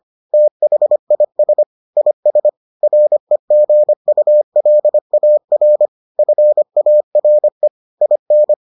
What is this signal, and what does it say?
Morse code, heard as a keyed tone, about 25 wpm, 600 Hz. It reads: THIS IS REGULAR FARE IN